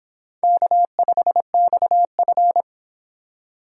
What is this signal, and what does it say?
Morse code, heard as a keyed tone, about 26 wpm, 700 Hz. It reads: K5XF